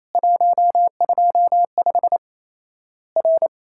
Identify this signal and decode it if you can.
Morse code, heard as a keyed tone, about 28 wpm, 700 Hz. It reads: 125 R
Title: Worked All Europe DX Contest - QTC Exchange - 28wpm